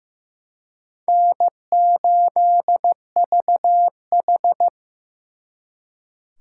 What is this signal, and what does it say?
Morse code, heard as a keyed tone, about 18 wpm, 700 Hz. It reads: N8VH
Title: Speed Racing - USA Call Signs 15wpm